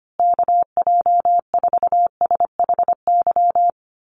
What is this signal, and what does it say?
Morse code, heard as a keyed tone, about 25 wpm, 700 Hz. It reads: KJ4SHY
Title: USA Call Signs Repeated 3x 25wpm